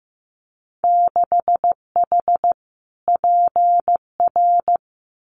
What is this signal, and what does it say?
Morse code, heard as a keyed tone, about 15 wpm, 700 Hz. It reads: 6H PR
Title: Winter Field Day Exchanges Encore 15wpm